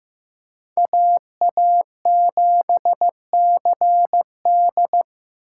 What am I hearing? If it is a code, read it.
Morse code, heard as a keyed tone, about 18 wpm, 700 Hz. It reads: AA7CD
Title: Speed Racing - USA 2x2 Call Signs 15wpm